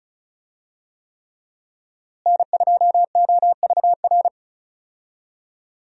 Morse code, heard as a keyed tone, about 36 wpm, 700 Hz. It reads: N2OVR
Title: USA Call Signs - 5 Characters - Repeated 3x - 35wpm